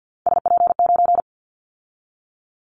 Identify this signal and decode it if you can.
Morse code, heard as a keyed tone, about 50 wpm, 700 Hz. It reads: SP9
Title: Callsign Prefixes for Top 25 DXCC - 50wpm